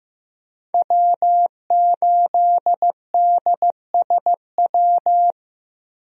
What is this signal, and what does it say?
Morse code, heard as a keyed tone, about 15 wpm, 700 Hz. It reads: W8DSW
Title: USA Call Signs - 5 Characters - 15wpm